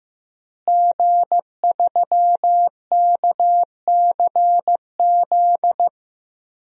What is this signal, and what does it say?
Morse code, heard as a keyed tone, about 15 wpm, 700 Hz. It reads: G3KCZ
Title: DXCC-100 Call Signs - 5 Characters - Repeated 2x - 15wpm